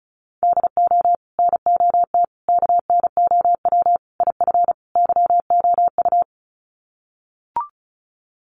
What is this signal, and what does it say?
Morse code, heard as a keyed tone, about 34 wpm, 700 Hz. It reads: DO NOT KNOW IF YOU
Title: Sets of 5 Words 35wpm